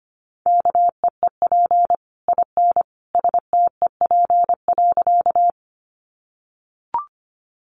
Morse code, heard as a keyed tone, about 25 wpm, 700 Hz. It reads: KEEP IN STEP.
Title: Sentences from Top 400 Words 25wpm